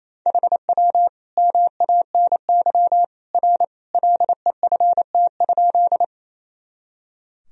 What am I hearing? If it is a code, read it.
Morse code, heard as a keyed tone, about 28 wpm, 700 Hz. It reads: HW MANY R LEFT?
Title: Sentences from Top 300 Words and Abbreviations 28wpm